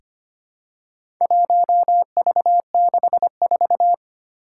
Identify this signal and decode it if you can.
Morse code, heard as a keyed tone, about 25 wpm, 700 Hz. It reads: 1V64
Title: Four Challenging Letter-Number Combinations 25wpm